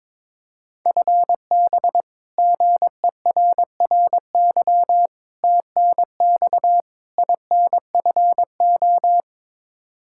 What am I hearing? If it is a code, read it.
Morse code, heard as a keyed tone, about 22 wpm, 700 Hz. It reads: FB GERRY TNX INFO
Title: Practice QSO - Exchange 3 - More Info - 22wpm